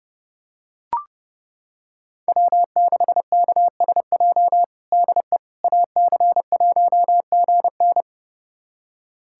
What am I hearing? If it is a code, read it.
Morse code, heard as a keyed tone, about 30 wpm, 700 Hz. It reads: W6KSJ DE AC1GN
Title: USA Callsign DE Callsign - 5 Characters - 30wpm